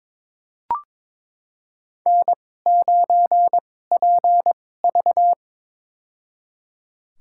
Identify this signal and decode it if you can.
Morse code, heard as a keyed tone, about 22 wpm, spaced as for 16 wpm, 700 Hz. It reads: N9PV
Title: USA Call Signs - 2x Character Spacing - 22wpm